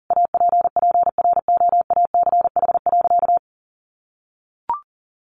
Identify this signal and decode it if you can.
Morse code, heard as a keyed tone, about 40 wpm, 700 Hz. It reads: APPROACH.
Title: Sentences from Top 1400 Words 40wpm